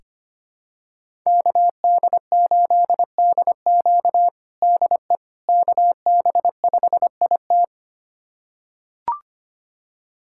Morse code, heard as a keyed tone, about 25 wpm, 700 Hz. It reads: KD8DQ DE KB5IT
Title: USA Callsign DE Callsign - 5 Characters - 25wpm